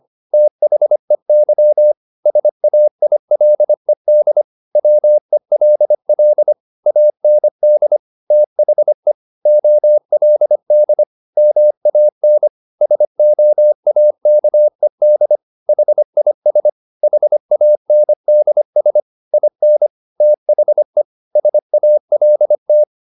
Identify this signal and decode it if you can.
Morse code, heard as a keyed tone, about 25 wpm, 600 Hz. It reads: THEY SAILED WELL AND THE OLD MAN SOAKED HIS HANDS IN THE SALT